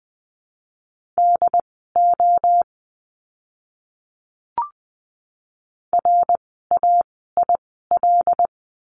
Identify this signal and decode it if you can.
Morse code, heard as a keyed tone, about 20 wpm, spaced as for 14 wpm, 700 Hz. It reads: DO RAIL
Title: 075 Lesson 13f - Letter L - Words 20wpm